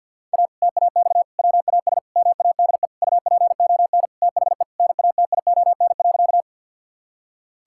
Morse code, heard as a keyed tone, about 52 wpm, 700 Hz. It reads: A TAX WAS MADE UPON THE NATION.
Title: Sentences from Top 1000 Words - Review Part 6 - 50wpm